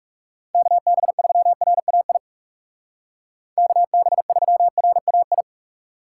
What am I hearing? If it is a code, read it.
Morse code, heard as a keyed tone, about 35 wpm, 700 Hz. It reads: KB3RAI KB3RAI
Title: Speed Racing - USA Call Signs 30wpm